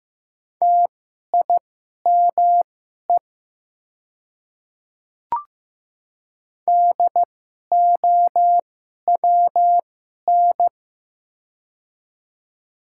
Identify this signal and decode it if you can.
Morse code, heard as a keyed tone, about 15 wpm, spaced as for 11 wpm, 700 Hz. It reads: TIME DOWN
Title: 2x Character Spacing - Top 100 Words - 15wpm